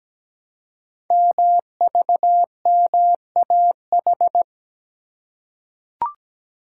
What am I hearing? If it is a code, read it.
Morse code, heard as a keyed tone, about 17 wpm, 700 Hz. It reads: MVMAH
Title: Code Groups - 5 Letters - 17wpm